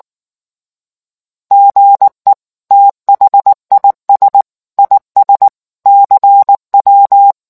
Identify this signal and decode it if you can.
Morse code, heard as a keyed tone, about 19 wpm, 800 Hz. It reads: GE THIS IS CW